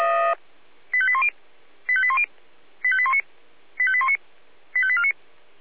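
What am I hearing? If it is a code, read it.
Five-tone selective call (ZVEI), heard as quick runs of five tones, tones of about 70 ms. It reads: 85810 85810 85819 85819 85849
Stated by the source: ZVEI five-tone call received by SDR.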